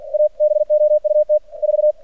Morse code, heard as a keyed tone, about 46 wpm, 600 Hz. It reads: ABOUT 4
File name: archived / CRSnet